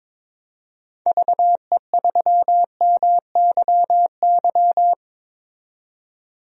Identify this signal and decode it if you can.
Morse code, heard as a keyed tone, about 22 wpm, 700 Hz. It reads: VE3MYY
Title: DXCC-100 Call Signs - 6 Characters - 22wpm